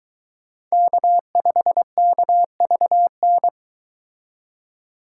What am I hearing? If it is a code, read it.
Morse code, heard as a keyed tone, about 23 wpm, 700 Hz. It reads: K5KVN